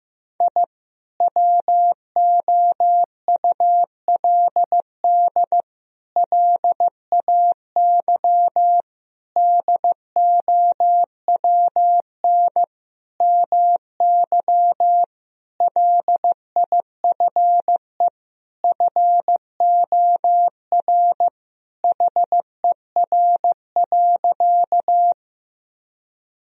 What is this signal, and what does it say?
Morse code, heard as a keyed tone, about 15 wpm, 700 Hz. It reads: I WOULD LAY DOWN MY LIFE FOR HER.
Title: Sentences from Top 1000 Words - Review Part 2 - 15wpm